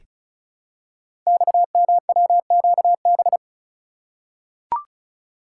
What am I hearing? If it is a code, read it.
Morse code, heard as a keyed tone, about 36 wpm, 700 Hz. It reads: XMWQB
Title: Code Groups - 5 Letter:Numbers - 35wpm